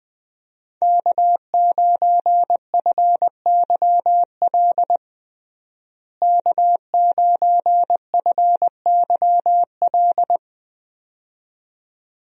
Morse code, heard as a keyed tone, about 21 wpm, 700 Hz. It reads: K9FYL K9FYL
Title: USA Call Signs - 5 Characters - Repeated 2x - 20wpm